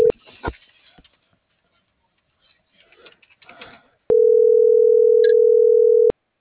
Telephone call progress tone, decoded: ringback tone

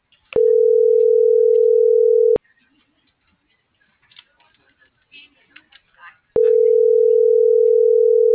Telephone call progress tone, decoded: ringback tone

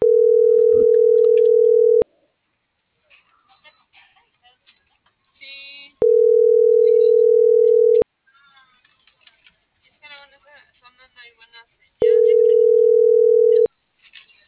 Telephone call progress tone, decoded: ringback tone

